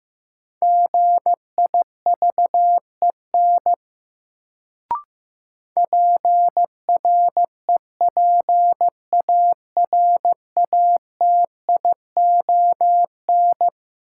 Morse code, heard as a keyed tone, about 15 wpm, 700 Hz. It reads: GIVEN PREPARATION
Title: Sentences from Top 1400 Words - New Word Drill - 15wpm